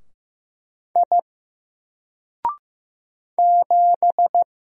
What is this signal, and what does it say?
Morse code, heard as a keyed tone, about 15 wpm, 700 Hz. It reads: I 7
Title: Single Character (Letter:Number:Punctuation:Prosign) Alt 15wpm